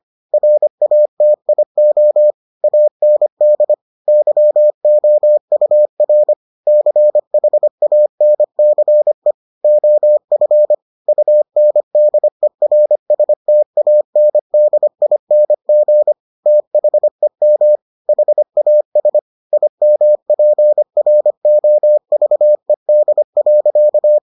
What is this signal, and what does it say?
Morse code, heard as a keyed tone, about 25 wpm, 600 Hz. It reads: RATIO AND YOUR CHANCE OF UNDERSTANDING THEM HAS IMPROVED.